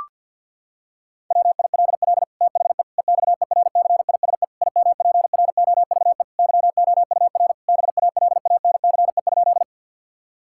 Morse code, heard as a keyed tone, about 52 wpm, 700 Hz. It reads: WILL THE EXERCISE IMPROVE YOUR BALANCE?
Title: Sentences from Top 1000 Words - Review Part 6 - 50wpm